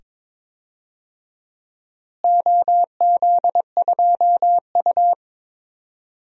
Morse code, heard as a keyed tone, about 21 wpm, 700 Hz. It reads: OZ2U
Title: DXCC-100 Call Signs - 4 Characters - Repeated 2x - 22wpm